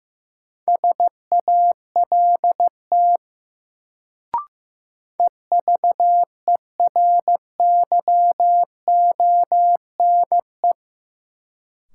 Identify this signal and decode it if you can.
Morse code, heard as a keyed tone, about 15 wpm, 700 Hz. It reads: SALT EVERYONE
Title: Sentences from Top 800 Words - New Word Drill - 15wpm